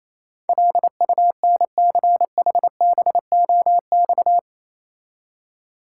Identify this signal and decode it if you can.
Morse code, heard as a keyed tone, about 27 wpm, 690 Hz. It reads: LUNCHBOX
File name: Easy Words - Speed-Racing with Hints - 22wpm